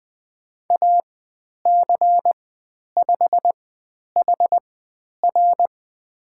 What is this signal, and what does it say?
Morse code, heard as a keyed tone, about 20 wpm, spaced as for 10 wpm, 700 Hz. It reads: AC5HR
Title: 109 Lesson 18g - Letter m - Call Signs 20wpm